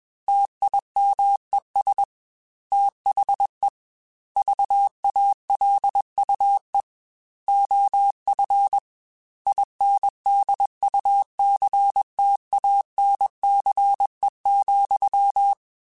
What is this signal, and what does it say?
Morse code, heard as a keyed tone, about 21 wpm, 800 Hz. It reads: TIMES THE VALUE OF INDUCTANCE,